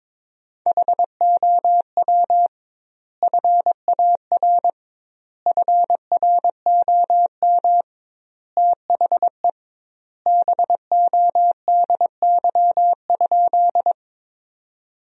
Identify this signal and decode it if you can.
Morse code, heard as a keyed tone, about 22 wpm, 700 Hz. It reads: HOW FAR FROM THE BODY?
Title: 2x Word-Spacing - Sentences from Top 400 Words - 22wpm